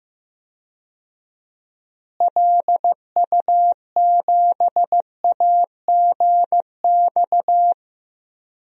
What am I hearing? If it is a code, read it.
Morse code, heard as a keyed tone, about 15 wpm, 700 Hz. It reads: LU7AGX
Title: DXCC-100 Call Signs - 6 Characters - Repeated 3x - 15wpm